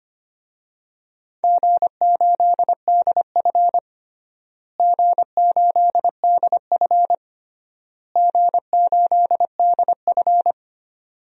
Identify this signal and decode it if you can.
Morse code, heard as a keyed tone, about 25 wpm, 700 Hz. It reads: G8DF G8DF G8DF
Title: DXCC-100 Call Signs - 4 Characters - Repeated 3x - 25wpm